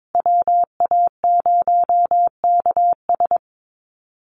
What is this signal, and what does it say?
Morse code, heard as a keyed tone, about 22 wpm, 700 Hz. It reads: WA0KS
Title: USA Call Signs Repeated 3x 22wpm